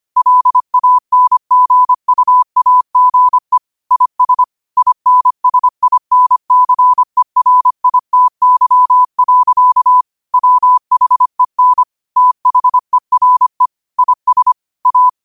Morse code, heard as a keyed tone, about 25 wpm, 1000 Hz. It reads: LANGUAGE IS INSINCERITY. WHEN THERE IS A
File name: qotdfiles / 25wpm